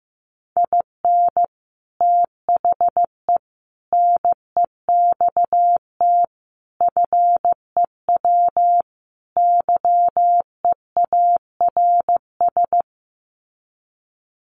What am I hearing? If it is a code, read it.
Morse code, heard as a keyed tone, about 15 wpm, 700 Hz. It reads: IN THE NEXT FEW YEARS
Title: Sets of 5 Words Encore 15wpm